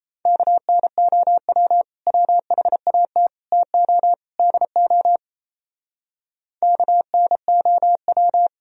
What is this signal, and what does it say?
Morse code, heard as a keyed tone, about 34 wpm, 700 Hz. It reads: KNOW WHAT TO DO KNOW
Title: Speed-Racing - Sets of 4 Words 28wpm